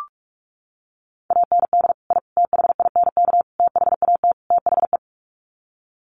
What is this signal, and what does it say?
Morse code, heard as a keyed tone, about 44 wpm, 700 Hz. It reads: AND I THINK THAT THE
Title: Sets of 5 Words 45wpm